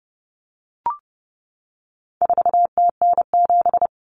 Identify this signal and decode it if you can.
Morse code, heard as a keyed tone, about 30 wpm, 700 Hz. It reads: 4TN7
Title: Common Cut Numbers - 4-Digit 30wpm